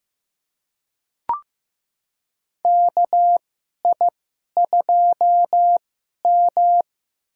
Morse code, heard as a keyed tone, about 15 wpm, spaced as for 11 wpm, 700 Hz. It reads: KI2M
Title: USA Call Signs - 2x Character Spacing - 15wpm